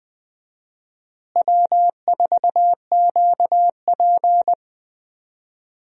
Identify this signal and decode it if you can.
Morse code, heard as a keyed tone, about 20 wpm, 700 Hz. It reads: W4QP
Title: USA Call Signs - 4 Characters - Repeated 3x - 20wpm